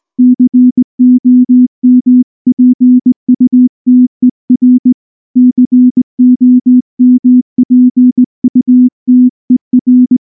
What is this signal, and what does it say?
Morse code, heard as a keyed tone, about 19 wpm, 250 Hz. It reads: COMPUTER COMPUTER
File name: morse / morse-training-20WPM-8